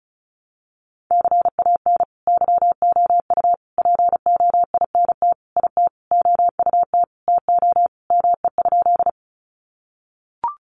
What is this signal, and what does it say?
Morse code, heard as a keyed tone, about 34 wpm, 700 Hz. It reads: CAN YOU POINT IT OUT TO ME?
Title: Sentences from Top 200 Words 35wpm